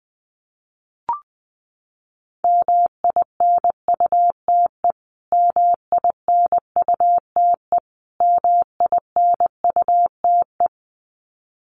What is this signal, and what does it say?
Morse code, heard as a keyed tone, about 20 wpm, 700 Hz. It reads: MINUTE MINUTE MINUTE
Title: Top 500 Words Repeated 20wpm